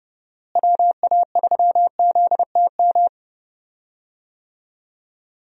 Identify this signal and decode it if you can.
Morse code, heard as a keyed tone, about 30 wpm, 700 Hz. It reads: WA3ZTM
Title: USA Call Signs - 6 Characters - Repeated 2x - 30wpm